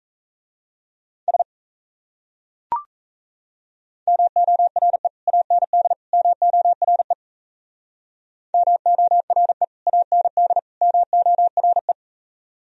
Morse code, heard as a keyed tone, about 37 wpm, 700 Hz. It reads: S MORE AND MORE MORE AND MORE